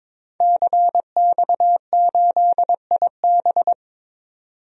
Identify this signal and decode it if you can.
Morse code, heard as a keyed tone, about 22 wpm, 700 Hz. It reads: CX8IB
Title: DXCC-100 Call Signs - 5 Characters - 22wpm